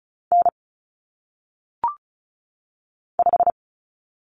Morse code, heard as a keyed tone, about 36 wpm, 700 Hz. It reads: N 5
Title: Common Cut Numbers - 1-Digit 35wpm